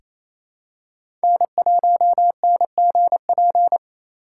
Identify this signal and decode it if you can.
Morse code, heard as a keyed tone, about 28 wpm, 700 Hz. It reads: N1NGP
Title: USA Call Signs - 5 Characters - 2x Character Spacing - 28wpm